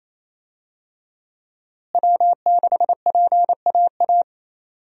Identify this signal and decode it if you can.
Morse code, heard as a keyed tone, about 28 wpm, 700 Hz. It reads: W6PAA
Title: USA Call Signs - 5 Characters - Repeated 2x - 28wpm